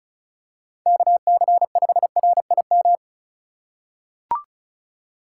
Morse code, heard as a keyed tone, about 36 wpm, 700 Hz. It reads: KC5RIM
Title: USA Call Signs - 6 Characters - 3x Character Spacing - 35wpm